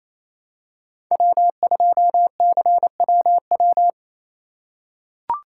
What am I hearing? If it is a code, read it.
Morse code, heard as a keyed tone, about 28 wpm, 700 Hz. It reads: W2CWW
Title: USA Call Signs - 5 Characters - 4x Character Spacing - 28wpm